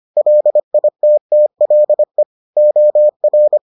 Morse code, heard as a keyed tone, about 25 wpm, 600 Hz. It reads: LITTLE OR